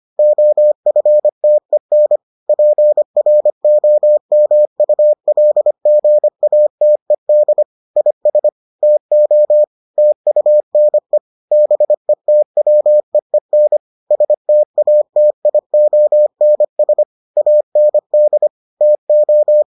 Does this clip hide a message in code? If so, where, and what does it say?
Yes: Morse code from 0.2 to 19.6 s, heard as a keyed tone, about 25 wpm, 600 Hz. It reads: OFTEN PROMULGATED IS TO TUNE BETWEEN STATIONS AND TO